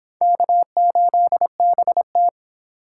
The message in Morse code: K8BT

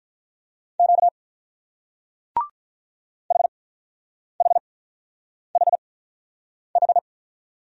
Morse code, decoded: Q HHHH

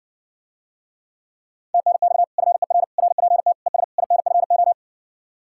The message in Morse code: TNX FER RPT ES INFO

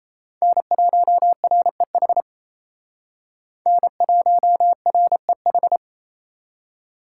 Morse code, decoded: N1REH N1REH

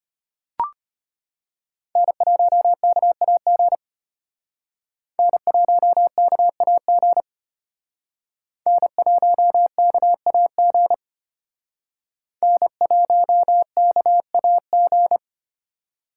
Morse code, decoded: N1KAG N1KAG N1KAG N1KAG

